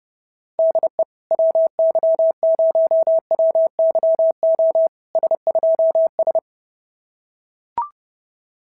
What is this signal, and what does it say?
Morse code, heard as a keyed tone, about 27 wpm, 650 Hz. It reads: DE WY0WYO S2S
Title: SOTA QSO Practice 30wpm